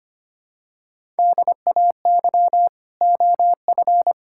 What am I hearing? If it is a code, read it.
Morse code, heard as a keyed tone, about 25 wpm, 700 Hz. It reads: DAY OF
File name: Sets of 9 Words - 25wpm